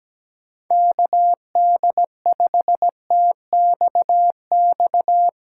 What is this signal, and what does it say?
Morse code, heard as a keyed tone, about 17 wpm, 700 Hz. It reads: KD5TXX
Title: K1USN SST QSO Practice - 17wpm